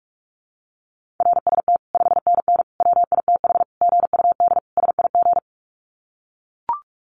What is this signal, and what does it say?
Morse code, heard as a keyed tone, about 44 wpm, 700 Hz. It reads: RST 5NN WITH GUD SIG
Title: Common QSO Phrases 45wpm